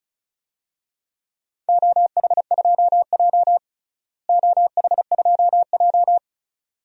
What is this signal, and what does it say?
Morse code, heard as a keyed tone, about 36 wpm, 700 Hz. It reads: OH2J OH2J
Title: DXCC-100 Call Signs - 4 Characters - Repeated 2x - 35wpm